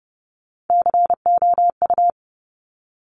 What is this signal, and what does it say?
Morse code, heard as a keyed tone, about 30 wpm, 700 Hz. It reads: COU
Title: Top Three Letter Combinations in Words 30wpm